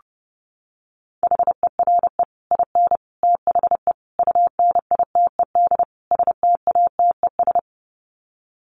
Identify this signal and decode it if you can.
Morse code, heard as a keyed tone, about 30 wpm, 700 Hz. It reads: HERE IN THE UNITED STATES
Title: Sets of 5 Words 30wpm